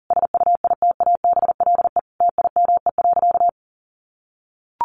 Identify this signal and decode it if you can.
Morse code, heard as a keyed tone, about 37 wpm, 700 Hz. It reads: SUITABLE TIME.
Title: Sentences from Top 1400 Words 40wpm